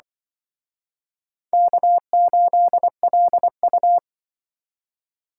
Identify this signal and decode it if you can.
Morse code, heard as a keyed tone, about 24 wpm, 700 Hz. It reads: K8LU
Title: Speed Racing - USA 1x2 Call Signs 20wpm